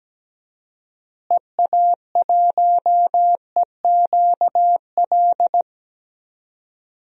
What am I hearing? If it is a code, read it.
Morse code, heard as a keyed tone, about 17 wpm, 700 Hz. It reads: EA1EQL
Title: DXCC-100 Call Signs - 6 Characters - Repeated 2x - 17wpm